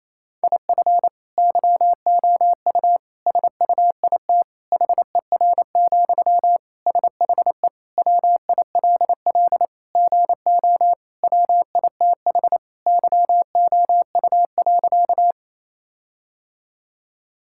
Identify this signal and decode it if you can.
Morse code, heard as a keyed tone, about 28 wpm, 700 Hz. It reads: IF YOU SUIT HER, SHE WILL GO WITH YOU.
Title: Sentences from Top 1600 Words 28wpm